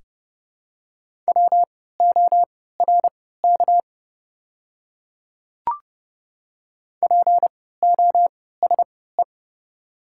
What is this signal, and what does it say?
Morse code, heard as a keyed tone, about 29 wpm, spaced as for 17 wpm, 700 Hz. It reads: WORK POSE
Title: 3x Character Spacing - Top 500 Words - 30wpm